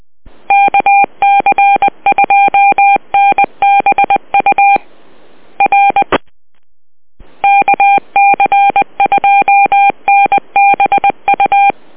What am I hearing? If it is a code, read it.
Morse code, heard as a keyed tone, about 20 wpm, 800 Hz. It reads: KC2NBU R KC2NBU